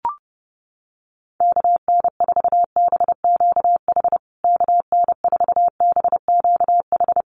Morse code, heard as a keyed tone, about 30 wpm, 700 Hz. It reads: KN4BQH KN4BQH
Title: USA Call Signs Repeated 2x 30wpm